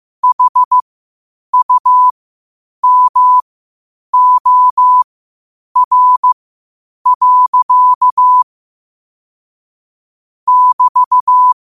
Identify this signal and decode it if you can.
Morse code, heard as a keyed tone, about 15 wpm, spaced as for 8 wpm, 1000 Hz. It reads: HUMOR. <BT>
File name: qotdfiles / 8wpm